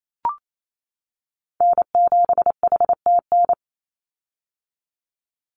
Morse code, heard as a keyed tone, about 28 wpm, 700 Hz. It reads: N7HTN
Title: USA Call Signs 28wpm